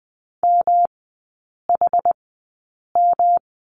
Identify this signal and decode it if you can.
Morse code, heard as a keyed tone, about 20 wpm, spaced as for 8 wpm, 700 Hz. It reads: MHM
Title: 107 Lesson 18e - Letter m - Three Characters 20wpm